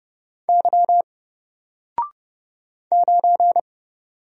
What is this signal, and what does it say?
Morse code, heard as a keyed tone, about 30 wpm, 700 Hz. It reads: Y 9
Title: Single Letter-Number Alt 30wpm